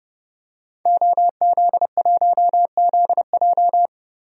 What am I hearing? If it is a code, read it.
Morse code, heard as a keyed tone, about 30 wpm, 700 Hz. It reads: OZ1ZJ
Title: DXCC-100 Call Signs - 5 Characters - Repeated 3x - 30wpm